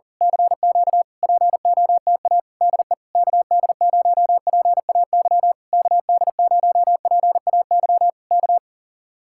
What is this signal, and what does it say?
Morse code, heard as a keyed tone, about 35 wpm, 700 Hz. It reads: CQ POTA DE KD0PAY KD0PAY K
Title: Speed-Racing Common QSO Phrases Encore - 40wpm